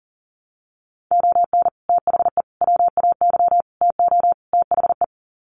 Morse code, heard as a keyed tone, about 40 wpm, 700 Hz. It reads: ON THE WAY TO THE
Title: Sets of 5 Words 40wpm